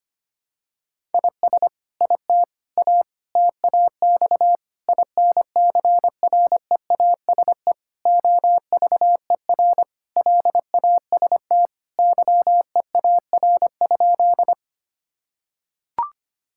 Morse code, heard as a keyed tone, about 25 wpm, 700 Hz. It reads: IS IT A TAX INCREASE OVER LAST YEAR?
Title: Sentences from Top 1000 Words - Review Part 5 - 25wpm